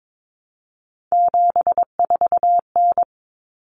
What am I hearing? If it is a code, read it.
Morse code, heard as a keyed tone, about 22 wpm, 700 Hz. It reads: 74N
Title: Common Cut Numbers - 3-Digit 22wpm